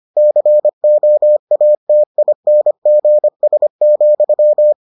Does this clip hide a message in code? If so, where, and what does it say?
Yes: Morse code from 0.2 to 4.7 s, heard as a keyed tone, about 25 wpm, 600 Hz. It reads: COATINGS,